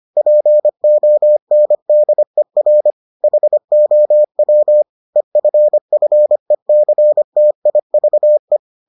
Morse code, heard as a keyed tone, about 25 wpm, 600 Hz. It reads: PONDER HOW EFFECTIVE